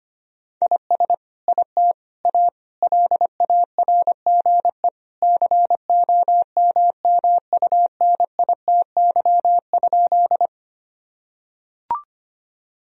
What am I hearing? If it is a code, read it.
Morse code, heard as a keyed tone, about 25 wpm, 700 Hz. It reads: IS IT A LARGE COMMUNITY?
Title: Sentences from Top 1000 Words - Review Part 4 - 25wpm